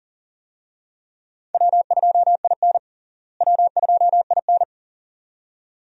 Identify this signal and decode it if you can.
Morse code, heard as a keyed tone, about 40 wpm, 700 Hz. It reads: W2IN W2IN
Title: USA Call Signs - 4 Characters - Repeated 2x - 40wpm